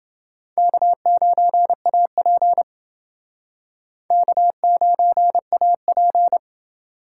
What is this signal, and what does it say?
Morse code, heard as a keyed tone, about 28 wpm, 700 Hz. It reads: K9AP K9AP